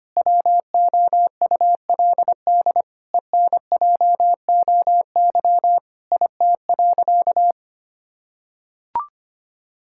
Morse code, heard as a keyed tone, about 25 wpm, 700 Hz. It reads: WOULD ENJOY IT.
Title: Sentences from Top 1000 Words - Review Part 5 - 25wpm